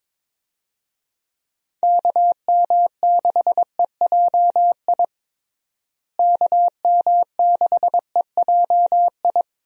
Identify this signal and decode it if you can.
Morse code, heard as a keyed tone, about 22 wpm, 700 Hz. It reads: KM6EJI KM6EJI
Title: USA Call Signs - 6 Characters - Repeated 3x - 22wpm